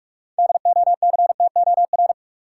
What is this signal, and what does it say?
Morse code, heard as a keyed tone, about 46 wpm, 700 Hz. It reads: DOCTOR
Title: Sentences from Top 1500 Words - New Word Drill Alt - 45wpm